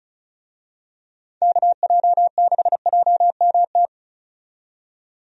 Morse code, heard as a keyed tone, about 34 wpm, 700 Hz. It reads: KJ6JMT
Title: USA Call Signs - 6 Characters - 35wpm